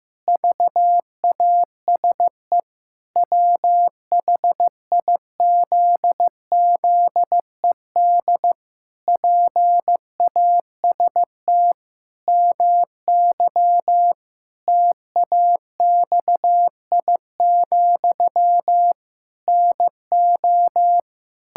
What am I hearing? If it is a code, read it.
Morse code, heard as a keyed tone, about 16 wpm, 700 Hz. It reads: VASE WHIZZED PAST MY TAXI, NO
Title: Pangram Challenge v2 - 15wpm